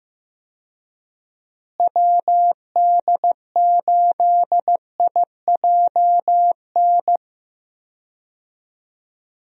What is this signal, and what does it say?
Morse code, heard as a keyed tone, about 15 wpm, 700 Hz. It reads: WD8IJN
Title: USA Call Signs - 6 Characters - Repeated 2x - 15wpm